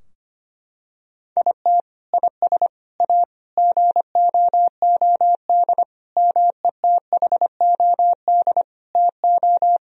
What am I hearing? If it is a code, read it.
Morse code, heard as a keyed tone, about 25 wpm, 700 Hz. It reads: IT IS A GOOD METHOD TO
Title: Sentences from Top 1000 Words - Review Part 3 - 25wpm